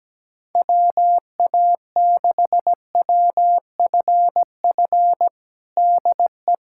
Morse code, heard as a keyed tone, about 17 wpm, 700 Hz. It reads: WA6WFF DE
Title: USA Callsign DE Callsign 17wpm